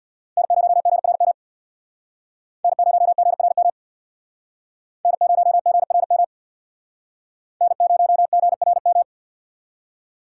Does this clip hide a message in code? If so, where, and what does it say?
Yes: Morse code from 0.4 to 9.0 s, heard as a keyed tone, about 64 wpm, 700 Hz. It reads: N0GRM N0GRM N0GRM N0GRM